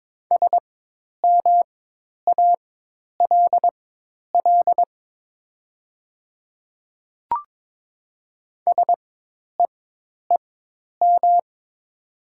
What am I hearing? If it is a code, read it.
Morse code, heard as a keyed tone, about 22 wpm, spaced as for 10 wpm, 700 Hz. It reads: SMALL SEEM
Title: 4x Character Spacing - Top 300 Words - 22wpm